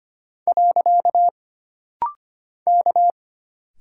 Morse code, heard as a keyed tone, about 25 wpm, 700 Hz. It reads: . K
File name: Single Character (Letter:Number:Punctuation:Prosign) - Rapid-Fire - 25wpm